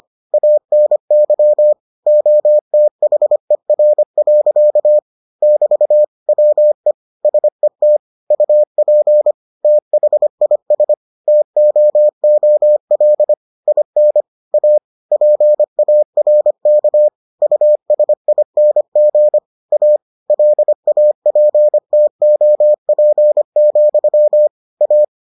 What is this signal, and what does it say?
Morse code, heard as a keyed tone, about 25 wpm, 600 Hz. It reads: ANY OTHER. <BT> WE SET UP THIS TOOL IN A PARK USING A LAPTOP, A